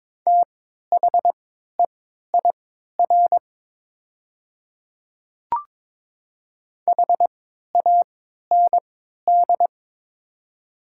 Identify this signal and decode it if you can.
Morse code, heard as a keyed tone, about 22 wpm, spaced as for 13 wpm, 700 Hz. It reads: THEIR HAND
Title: 3x Character Spacing - Top 300 Words - 22wpm